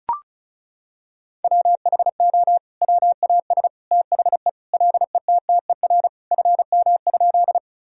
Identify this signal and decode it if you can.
Morse code, heard as a keyed tone, about 36 wpm, 700 Hz. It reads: WHO WAS THE LETTER FM?
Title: Sentences from Top 300 Words and Abbreviations 35wpm